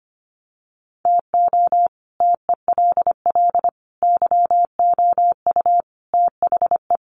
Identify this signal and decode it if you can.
Morse code, heard as a keyed tone, about 25 wpm, 700 Hz. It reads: TO TELL YOU THE
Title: Sets of 5 Words 25wpm